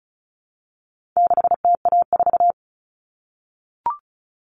Morse code, heard as a keyed tone, about 34 wpm, 700 Hz. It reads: 6TA4
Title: All Cut Numbers - 4-Digit 35wpm